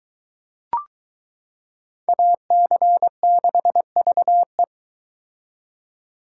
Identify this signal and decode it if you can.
Morse code, heard as a keyed tone, about 22 wpm, 700 Hz. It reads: AC6VE